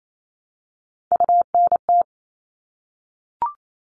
Morse code, heard as a keyed tone, about 28 wpm, 700 Hz. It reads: UNT
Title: All Cut Numbers - 3-Digit 28wpm